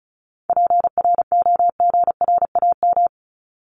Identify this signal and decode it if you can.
Morse code, heard as a keyed tone, about 34 wpm, 700 Hz. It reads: PROGRAM
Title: Top 501-750 Words 35wpm